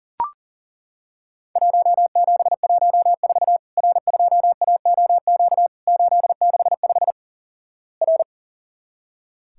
Morse code, heard as a keyed tone, about 42 wpm, 700 Hz. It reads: 1714 R2AOQ 865 R
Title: Worked All Europe DX Contest - QTC Exchange - 40wpm